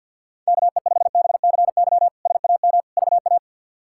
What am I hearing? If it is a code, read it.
Morse code, heard as a keyed tone, about 52 wpm, 700 Hz. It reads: KE5BCY SAM VA
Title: K1USN SST QSO Practice - 50wpm